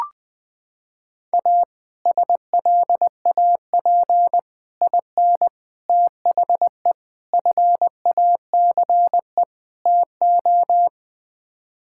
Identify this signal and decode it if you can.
Morse code, heard as a keyed tone, about 20 wpm, 700 Hz. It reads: A SLAP IN THE FACE TO
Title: Sets of 6 Words - 20wpm